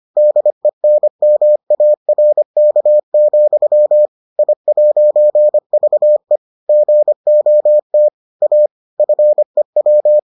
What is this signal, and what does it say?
Morse code, heard as a keyed tone, about 25 wpm, 600 Hz. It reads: DENMARK, I'VE GOT A FEW